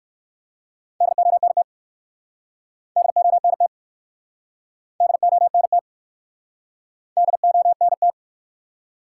Morse code, heard as a keyed tone, about 58 wpm, 700 Hz. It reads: DONT DONT DONT DONT